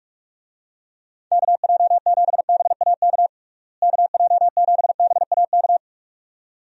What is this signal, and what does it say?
Morse code, heard as a keyed tone, about 41 wpm, 700 Hz. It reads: KJ7BAK KJ7BAK
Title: USA Call Signs - 6 Characters - Repeated 2x - 45wpm